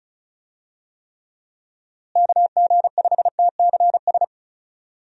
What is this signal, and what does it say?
Morse code, heard as a keyed tone, about 36 wpm, 700 Hz. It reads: KG5TCS
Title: USA Call Signs - 6 Characters - Repeated 2x - 35wpm